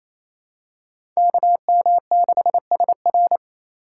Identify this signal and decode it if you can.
Morse code, heard as a keyed tone, about 28 wpm, 700 Hz. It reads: KM6SR
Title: USA Call Signs - 5 Characters - 2x Character Spacing - 28wpm